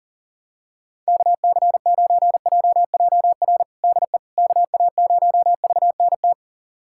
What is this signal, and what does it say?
Morse code, heard as a keyed tone, about 40 wpm, 700 Hz. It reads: KC9JJR DE KA0VNT
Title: USA Callsign DE Callsign - 6 Characters - 40wpm